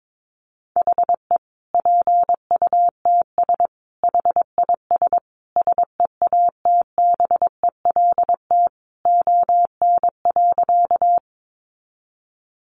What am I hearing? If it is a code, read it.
Morse code, heard as a keyed tone, about 22 wpm, 700 Hz. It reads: HE PUTS HIS SEATBELT ON.
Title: Additional Easy Phrases 22wpm